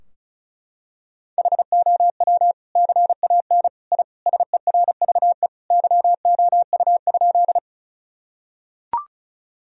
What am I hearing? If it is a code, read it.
Morse code, heard as a keyed tone, about 36 wpm, 700 Hz. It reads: HOW CAN I SERVE YOU?
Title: Sentences from Top 1000 Words - Review Part 2 - 35wpm